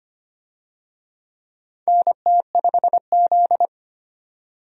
Morse code, heard as a keyed tone, about 25 wpm, 700 Hz. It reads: NT5Z
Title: USA Call Signs - 4 Characters - 25wpm